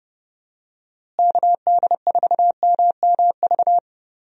Morse code, heard as a keyed tone, about 30 wpm, 700 Hz. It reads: KD4MMV
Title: USA Call Signs - 6 Characters - Repeated 3x - 30wpm